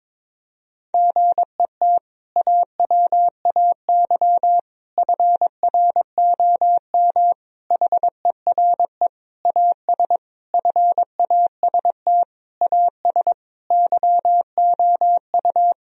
Morse code, heard as a keyed tone, about 22 wpm, 700 Hz. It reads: GET AWAY FROM HERE AS FAST AS YOU
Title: Sentences from Top 1000 Words - Review Part 2 - 22wpm